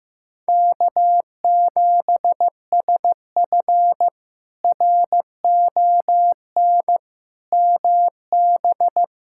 Morse code, heard as a keyed tone, about 15 wpm, 700 Hz. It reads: K7SF RON MB
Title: K1USN SST QSO Practice - 15wpm